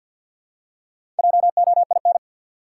Morse code, heard as a keyed tone, about 53 wpm, 700 Hz. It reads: JOIN
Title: Four Letter Words - 50wpm